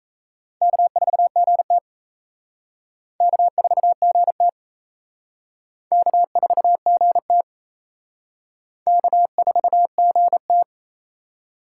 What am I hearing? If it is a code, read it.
Morse code, heard as a keyed tone, about 35 wpm, 700 Hz. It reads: K4GT K4GT K4GT K4GT
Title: Speed Racing - USA 1x2 Call Signs 28wpm